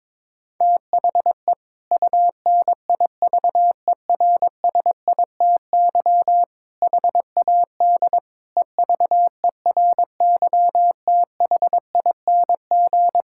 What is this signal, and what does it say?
Morse code, heard as a keyed tone, about 22 wpm, 700 Hz. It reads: THE UNIVERSITY HAD EVERYTHING
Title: Sentences from Top 1000 Words - Review Part 5 - 22wpm